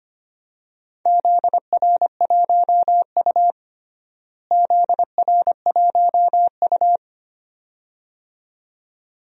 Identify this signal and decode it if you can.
Morse code, heard as a keyed tone, about 25 wpm, 700 Hz. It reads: ZR1U ZR1U
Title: DXCC-100 Call Signs - 4 Characters - Repeated 2x - 25wpm